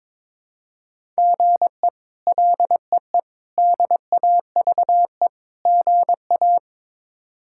Morse code, heard as a keyed tone, about 22 wpm, 700 Hz. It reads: GE LEE DAVE GA
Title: K1USN SST QSO Practice - 22wpm